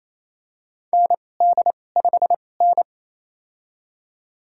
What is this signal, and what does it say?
Morse code, heard as a keyed tone, about 28 wpm, spaced as for 20 wpm, 700 Hz. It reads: ND5N
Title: USA Call Signs - 2x Character Spacing - 28wpm